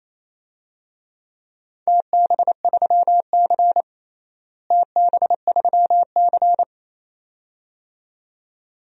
Morse code, heard as a keyed tone, about 28 wpm, 700 Hz. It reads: TB3C TB3C
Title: DXCC-100 Call Signs - 4 Characters - Repeated 2x - 28wpm